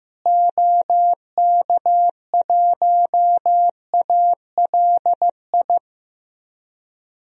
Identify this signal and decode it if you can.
Morse code, heard as a keyed tone, about 16 wpm, 700 Hz. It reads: OK1ALI
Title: DXCC-100 Call Signs - 6 Characters - Repeated 3x - 15wpm